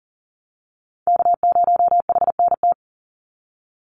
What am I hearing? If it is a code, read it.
Morse code, heard as a keyed tone, about 40 wpm, 700 Hz. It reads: K0HNT
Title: USA Call Signs 40wpm